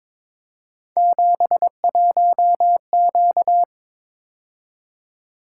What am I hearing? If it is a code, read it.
Morse code, heard as a keyed tone, about 22 wpm, 700 Hz. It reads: 71Q
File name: practice / 22wpm / Three Letter-Number Combinations - 22wpm